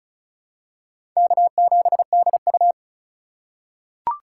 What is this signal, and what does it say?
Morse code, heard as a keyed tone, about 36 wpm, 700 Hz. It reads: K7DU
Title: USA Call Signs - 4 Characters - 3x Character Spacing - 35wpm